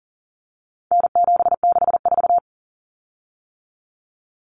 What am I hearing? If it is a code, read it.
Morse code, heard as a keyed tone, about 40 wpm, 700 Hz. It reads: N764
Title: Common Cut Numbers - 4-Digit 40wpm